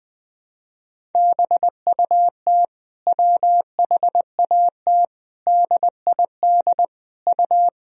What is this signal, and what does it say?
Morse code, heard as a keyed tone, about 20 wpm, 700 Hz. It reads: BUT WHAT DID U
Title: Sentences from Top 100 Words and Abbreviations 20wpm